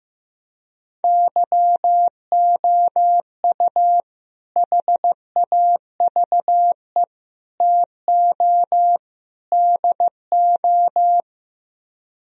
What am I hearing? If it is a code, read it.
Morse code, heard as a keyed tone, about 16 wpm, 700 Hz. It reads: YOU HAVE TO DO
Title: Speed-Racing - Sets of 4 Words 15wpm